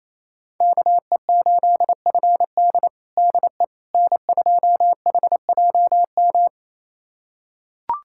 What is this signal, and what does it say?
Morse code, heard as a keyed tone, about 28 wpm, 700 Hz. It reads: KE8FD DE N2HJM
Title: USA Callsign DE Callsign - 5 Characters - 28wpm